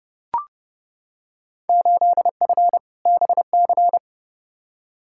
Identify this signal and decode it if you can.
Morse code, heard as a keyed tone, about 30 wpm, 700 Hz. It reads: 8F BC
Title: Field Day Exchanges Encore - 30wpm